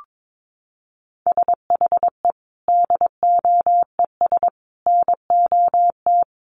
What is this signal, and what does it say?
Morse code, heard as a keyed tone, about 22 wpm, 700 Hz. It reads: SHE DOES NOT
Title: Sets of 5 Words Encore 22wpm